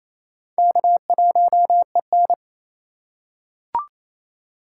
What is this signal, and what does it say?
Morse code, heard as a keyed tone, about 27 wpm, 700 Hz. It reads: K1EN
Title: USA Call Signs - 4 Characters - 2x Character Spacing - 28wpm